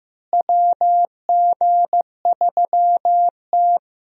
Morse code, heard as a keyed tone, about 15 wpm, 700 Hz. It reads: WG3T
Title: USA Call Signs - 4 Characters - Repeated 3x - 15wpm